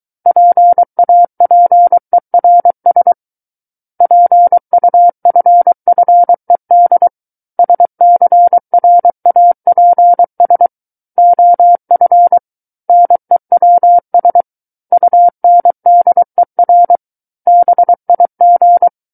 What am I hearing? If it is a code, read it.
Morse code, heard as a keyed tone, about 23 wpm, 700 Hz. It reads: PAPERS PUFFED SCRAPS OF NEWS UNDER BIG
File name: Book One / 14-Chapter 14 - 23 WPM